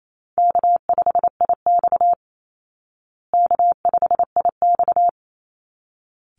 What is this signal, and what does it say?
Morse code, heard as a keyed tone, about 28 wpm, 700 Hz. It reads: K5IX K5IX
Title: Easy Call Signs Repeated 2x - 28wpm